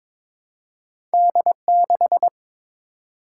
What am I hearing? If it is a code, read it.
Morse code, heard as a keyed tone, about 22 wpm, 700 Hz. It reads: D6
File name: practice / 22wpm / Two Letter-Number Combinations - 22wpm